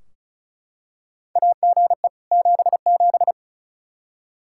Morse code, heard as a keyed tone, about 36 wpm, 700 Hz. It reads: AGE 77
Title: Practice QSO - Exchange 3 - More Info - 35wpm